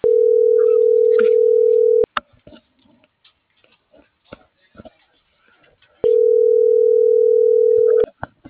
Ringback tone (North America), 440 + 480 Hz.
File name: CENTRAL AGUILERA / out brenda.ramos